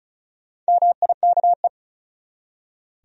Morse code, heard as a keyed tone, about 35 wpm, 700 Hz. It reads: MIKE
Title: USA Common Names - Max 4 Letters - 35wpm